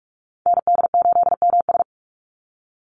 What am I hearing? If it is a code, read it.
Morse code, heard as a keyed tone, about 44 wpm, 700 Hz. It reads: ND8MS
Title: Call Signs Repeated 2x - 45wpm